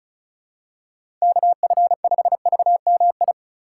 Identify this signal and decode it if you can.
Morse code, heard as a keyed tone, about 36 wpm, 700 Hz. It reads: KF5VMI
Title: USA Call Signs - 6 Characters - 4x Character Spacing - 35wpm